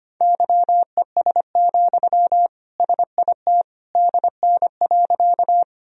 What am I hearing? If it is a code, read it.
Morse code, heard as a keyed tone, about 25 wpm, 700 Hz. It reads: YES, SIT DN.
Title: Sentences from Top 500 Words and Abbreviations 25wpm